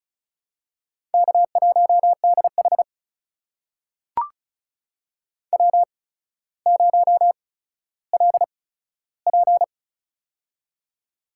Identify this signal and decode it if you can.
Morse code, heard as a keyed tone, about 36 wpm, spaced as for 15 wpm, 700 Hz. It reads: K1DH W0LP
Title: USA Call Signs - 4 Characters - 8x Character Spacing - 35wpm